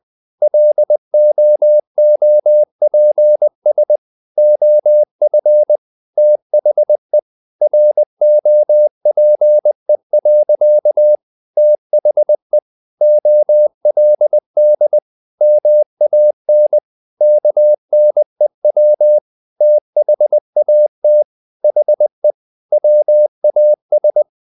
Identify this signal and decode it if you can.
Morse code, heard as a keyed tone, about 20 wpm, 600 Hz. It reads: LOOPS OF THE ROPE. THE OLD MAN KNEW THAT HE WAS
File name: the old man and the sea part 2 20wpm